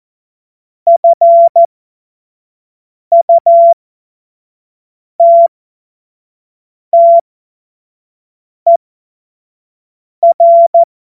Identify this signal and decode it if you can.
Morse code, heard as a keyed tone, about 14 wpm, spaced as for 5 wpm, 680 Hz. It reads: FUTTER
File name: Morsekurs / Witze / Witz 07. September 2021, 25 F70